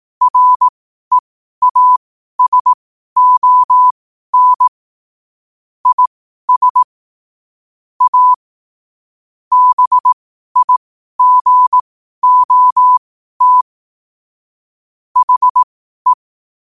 Morse code, heard as a keyed tone, about 18 wpm, spaced as for 12 wpm, 1000 Hz. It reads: REASON IS A BIGOT HE
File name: qotdfiles / 12wpm